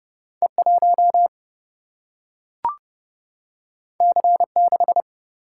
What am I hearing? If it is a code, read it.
Morse code, heard as a keyed tone, about 29 wpm, 700 Hz. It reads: E 1 C 6
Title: Single-Letter Single-Number 30wpm